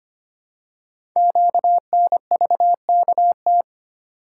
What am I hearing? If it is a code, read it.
Morse code, heard as a keyed tone, about 25 wpm, 700 Hz. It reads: QNVKT
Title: Code Groups - 5 Letter:Numbers - 25wpm